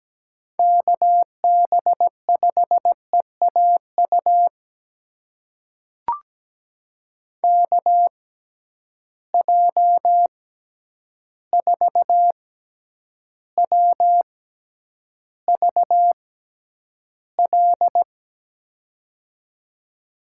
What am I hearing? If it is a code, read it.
Morse code, heard as a keyed tone, about 17 wpm, spaced as for 9 wpm, 700 Hz. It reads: KB5EAU KJ4WVL
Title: USA Call Signs - 6 Characters - 6x Character Spacing - 17wpm